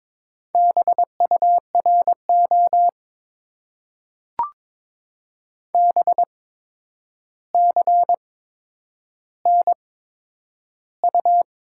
Morse code, heard as a keyed tone, about 22 wpm, spaced as for 9 wpm, 700 Hz. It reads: BURO BCNU
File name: Top 90 QSO Elements - 8x Character Spacing - 22wpm